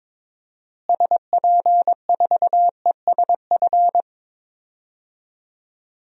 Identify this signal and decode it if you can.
Morse code, heard as a keyed tone, about 21 wpm, 700 Hz. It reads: SP4ESF
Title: DXCC-100 Call Signs - 6 Characters - Repeated 3x - 22wpm